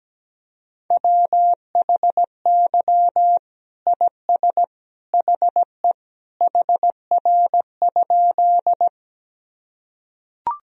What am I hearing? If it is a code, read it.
Morse code, heard as a keyed tone, about 17 wpm, 700 Hz. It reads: WHY IS HE HR?
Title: Sentences from Top 200 Words and Abbreviations 17wpm